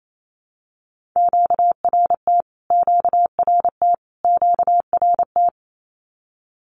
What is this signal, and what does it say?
Morse code, heard as a keyed tone, about 28 wpm, 700 Hz. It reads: QRT QRT QRT
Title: Q-Codes Repeated 3x 28wpm